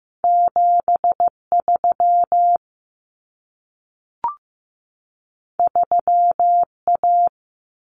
Morse code, heard as a keyed tone, about 15 wpm, 700 Hz. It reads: 73 3A
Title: Common Cut Numbers - 2-Digit 15wpm